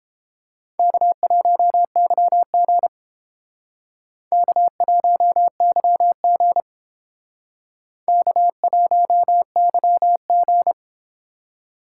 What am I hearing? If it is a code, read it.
Morse code, heard as a keyed tone, about 31 wpm, 700 Hz. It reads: K1YG K1YG K1YG